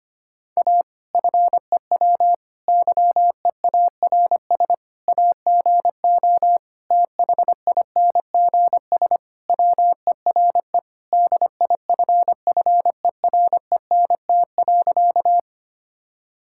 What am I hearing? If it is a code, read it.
Morse code, heard as a keyed tone, about 25 wpm, 700 Hz. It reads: A FEW YEARS AGO THINGS WERE DIFFERENT.
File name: practice / 25wpm / Sentences with 4-Word Phrases - 25wpm